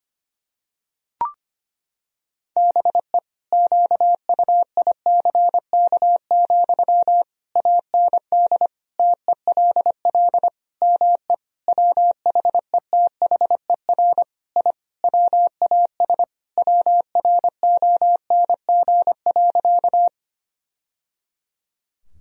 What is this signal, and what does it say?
Morse code, heard as a keyed tone, about 25 wpm, 700 Hz. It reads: BE QUICK, AND TELL ME WHETHER I WAS WRONG.
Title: Sentences from Top 1000 Words - Review Part 3 - 25wpm